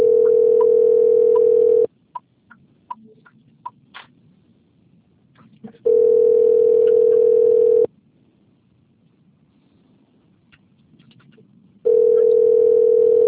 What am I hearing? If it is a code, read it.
Ringback tone (North America), 440 + 480 Hz.